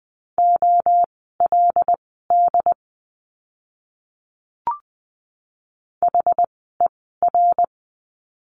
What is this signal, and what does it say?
Morse code, heard as a keyed tone, about 20 wpm, spaced as for 14 wpm, 700 Hz. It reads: OLD HER
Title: 230 Lesson 37f - Letter x - Words 20wpm